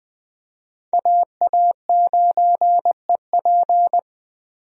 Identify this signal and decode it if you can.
Morse code, heard as a keyed tone, about 20 wpm, 700 Hz. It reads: AA9EP
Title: USA Call Signs - 5 Characters - 20wpm